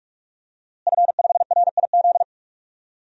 Morse code, heard as a keyed tone, about 41 wpm, 700 Hz. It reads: F5RIZ
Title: DXCC-100 Call Signs - 5 Characters - Repeated 3x - 45wpm